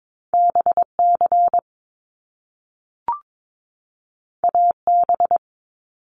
Morse code, heard as a keyed tone, about 22 wpm, 700 Hz. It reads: BC AB
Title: Canadian Provinces and Territories 22wpm